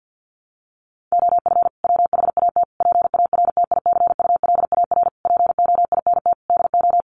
Morse code, heard as a keyed tone, about 50 wpm, 700 Hz. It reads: OF WHAT PARTICULAR POINT DO
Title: Sentences from Top 1300 Words 50wpm